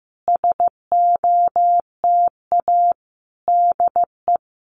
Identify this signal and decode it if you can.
Morse code, heard as a keyed tone, about 15 wpm, 700 Hz. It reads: SOTA DE
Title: Common QSO Phrases Encore 15wpm